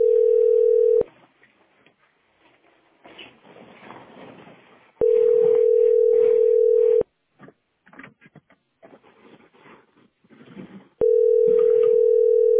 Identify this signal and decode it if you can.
Ringback tone (North America), 440 + 480 Hz.